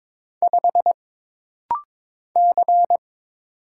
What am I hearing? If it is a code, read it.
Morse code, heard as a keyed tone, about 22 wpm, 700 Hz. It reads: 5 C
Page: This rapid-fire timing is geared to push your ICR (Instant Character Recognition) skills.